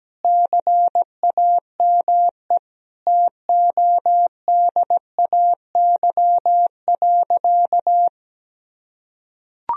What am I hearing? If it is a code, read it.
Morse code, heard as a keyed tone, about 17 wpm, 700 Hz. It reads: CAME TODAY.
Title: Sentences from Top 1000 Words - Review Part 3 - 17wpm